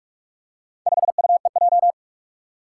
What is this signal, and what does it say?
Morse code, heard as a keyed tone, about 42 wpm, 700 Hz. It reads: 5UEJ
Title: Four Challenging Letter-Number Combinations 45wpm